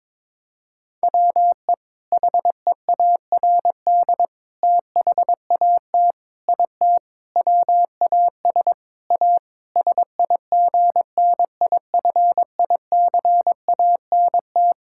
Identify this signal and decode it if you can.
Morse code, heard as a keyed tone, about 22 wpm, 700 Hz. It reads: WE HEARD THAT IT WAS A SIGNIFICANT
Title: Sentences from Top 1000 Words - Review Part 6 - 22wpm